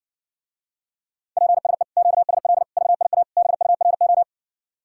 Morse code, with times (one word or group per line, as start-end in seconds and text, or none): 1.4-1.8 PSE
2.0-2.6 QSL
2.8-3.2 VIA
3.4-4.2 BURO